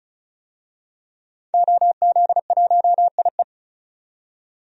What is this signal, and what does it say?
Morse code, heard as a keyed tone, about 36 wpm, 700 Hz. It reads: OZ1IE
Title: DXCC-100 Call Signs - 5 Characters - 35wpm